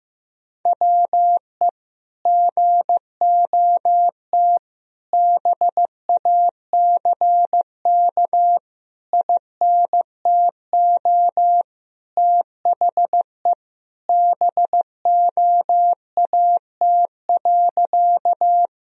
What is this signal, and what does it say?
Morse code, heard as a keyed tone, about 15 wpm, 700 Hz. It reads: WE GOT BACK INTO THE BOAT.
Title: Sentences from Top 1000 Words - Review Part 3 - 15wpm